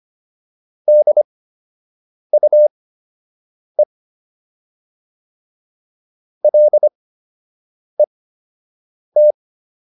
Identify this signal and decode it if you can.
Morse code, heard as a keyed tone, about 25 wpm, spaced as for 7 wpm, 600 Hz. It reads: DUE LET